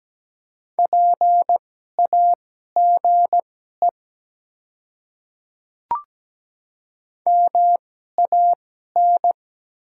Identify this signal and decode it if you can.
Morse code, heard as a keyed tone, about 17 wpm, spaced as for 12 wpm, 700 Hz. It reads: PAGE MAN
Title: 2x Character Spacing - Top 300 Words - 17wpm